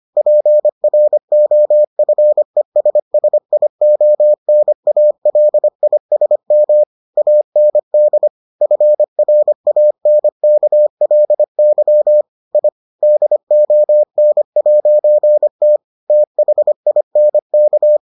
Morse code, heard as a keyed tone, about 25 wpm, 600 Hz. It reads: PROFESSIONALISM AND FRANKLY I DON'T THINK